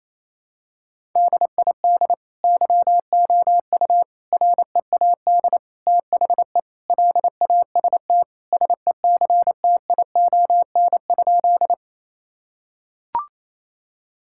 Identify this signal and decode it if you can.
Morse code, heard as a keyed tone, about 28 wpm, 700 Hz. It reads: DID YOU READ THE LAST SECTION?
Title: Sentences from Top 1000 Words - Review Part 5 - 28wpm